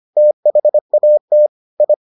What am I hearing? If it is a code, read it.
Morse code, heard as a keyed tone, about 25 wpm, 600 Hz. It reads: THAT I